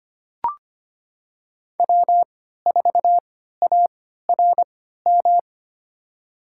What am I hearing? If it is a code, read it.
Morse code, heard as a keyed tone, about 25 wpm, spaced as for 14 wpm, 700 Hz. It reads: W4ARM
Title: USA Call Signs - 5 Characters - 3x Character Spacing - 25wpm